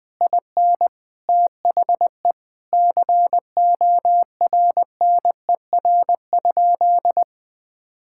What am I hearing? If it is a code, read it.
Morse code, heard as a keyed tone, about 21 wpm, 700 Hz. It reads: IN THE CORNER?
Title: Sentences from Top 1600 Words 20wpm